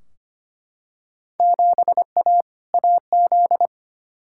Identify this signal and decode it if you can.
Morse code, heard as a keyed tone, about 25 wpm, 700 Hz. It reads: 7A AZ
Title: Field Day Exchanges - 25wpm